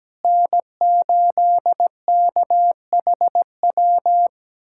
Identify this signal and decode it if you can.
Morse code, heard as a keyed tone, about 17 wpm, 700 Hz. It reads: N8KHW
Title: USA Callsign DE Callsign - 5 Characters - 17wpm